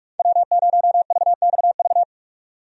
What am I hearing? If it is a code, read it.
Morse code, heard as a keyed tone, about 41 wpm, 700 Hz. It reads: W0VXV
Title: USA Call Signs - 5 Characters - 45wpm